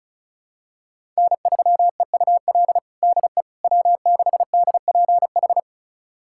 Morse code, heard as a keyed tone, about 34 wpm, 700 Hz. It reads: N3EUL DE W6DPH
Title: USA Callsign DE Callsign - 5 Characters - 35wpm